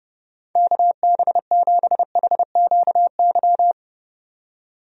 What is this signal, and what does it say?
Morse code, heard as a keyed tone, about 32 wpm, 700 Hz. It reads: KB7HQY
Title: USA Call Signs - 6 Characters - 30wpm